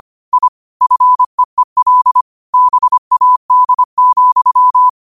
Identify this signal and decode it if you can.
Morse code, heard as a keyed tone, about 25 wpm, 1000 Hz. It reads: I FEEL BAD,